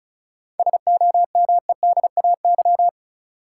Morse code, heard as a keyed tone, about 34 wpm, 700 Hz. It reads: SOMEDAY
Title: Sentences from Top 1600 Words - New Word Drill - 35wpm